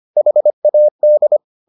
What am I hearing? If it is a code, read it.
Morse code, heard as a keyed tone, about 25 wpm, 600 Hz. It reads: HAD